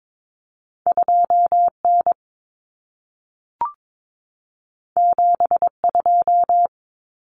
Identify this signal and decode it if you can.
Morse code, heard as a keyed tone, about 22 wpm, 700 Hz. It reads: 2N 72
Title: Common Cut Numbers - 2-Digit 22wpm